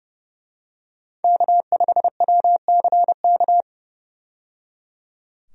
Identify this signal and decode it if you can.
Morse code, heard as a keyed tone, about 30 wpm, 700 Hz. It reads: K5WCK
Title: USA Call Signs - 5 Characters - 30wpm